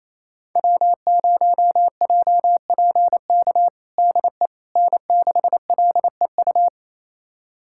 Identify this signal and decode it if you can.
Morse code, heard as a keyed tone, about 28 wpm, 700 Hz. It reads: W0JPK DE N6LEU
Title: USA Callsign DE Callsign - 5 Characters - 28wpm